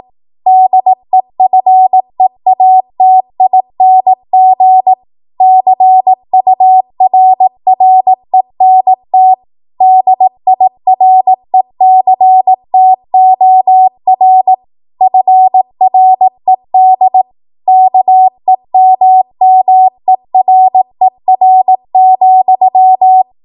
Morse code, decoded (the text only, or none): DEFEATING CURRENT DIRECTOR FRED KEMMERER,